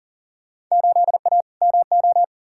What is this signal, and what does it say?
Morse code, heard as a keyed tone, about 40 wpm, 700 Hz. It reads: 8A MO
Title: Field Day Exchanges Encore - 40wpm